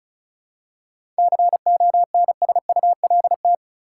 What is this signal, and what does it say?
Morse code, heard as a keyed tone, about 34 wpm, 700 Hz. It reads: CONSULT
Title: Seven Letter Words - 35wpm